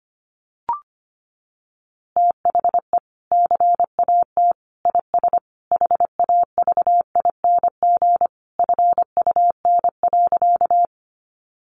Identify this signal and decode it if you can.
Morse code, heard as a keyed tone, about 25 wpm, 700 Hz. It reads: THE CAT IS HAVING FUN.
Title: Easy Phrases 25wpm